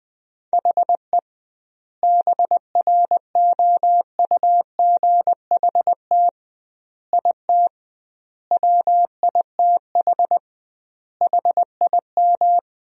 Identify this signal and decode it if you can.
Morse code, heard as a keyed tone, about 20 wpm, 700 Hz. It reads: HE BROUGHT IT WITH HIM
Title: 2x Word-Spacing - Sentences from Top 500 Words - 20wpm